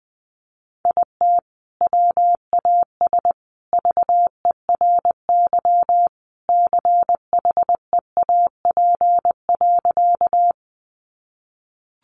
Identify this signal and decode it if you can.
Morse code, heard as a keyed tone, about 19 wpm, 700 Hz. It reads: IT WAS VERY CHEAP.
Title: Sentences from Top 1200 Words 20wpm